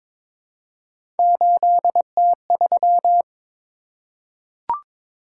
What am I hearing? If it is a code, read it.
Morse code, heard as a keyed tone, about 23 wpm, 700 Hz. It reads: 8T3
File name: practice / 22wpm / Three Letter-Number Combinations - 22wpm